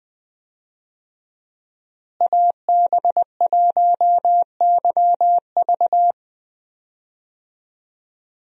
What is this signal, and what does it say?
Morse code, heard as a keyed tone, about 20 wpm, 700 Hz. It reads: AB1YV
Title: USA Call Signs - 5 Characters - Repeated 2x - 20wpm